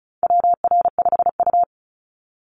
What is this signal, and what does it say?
Morse code, heard as a keyed tone, about 36 wpm, 700 Hz. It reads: WR5U
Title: Call Signs - 35wpm